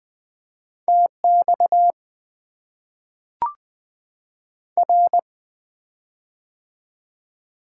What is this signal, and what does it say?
Morse code, heard as a keyed tone, about 20 wpm, 690 Hz. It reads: TX R
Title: Top 90 QSO Elements - 4x Character Spacing - 20wpm